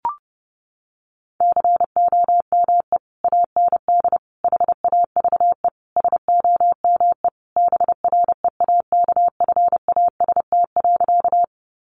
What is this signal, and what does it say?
Morse code, heard as a keyed tone, about 30 wpm, 700 Hz. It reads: COME AND HAVE SOME BREAKFAST.
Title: Sentences from Top 1500 Words 30wpm